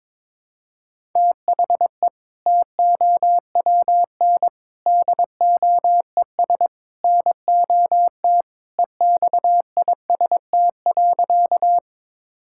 Morse code, heard as a keyed tone, about 22 wpm, 700 Hz. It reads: THE TOWN DOES NOT EXIST.
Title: Sentences from Top 1000 Words - Review Part 5 - 22wpm